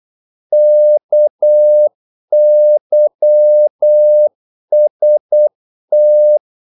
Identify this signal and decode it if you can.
Morse code, heard as a keyed tone, about 8 wpm, 600 Hz. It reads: KYST
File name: Korte ord DK 8wpm